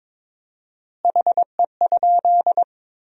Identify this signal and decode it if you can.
Morse code, heard as a keyed tone, about 22 wpm, 700 Hz. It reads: HE?
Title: 4x Word-Spacing - Sentences from Top 100 Words - 22wpm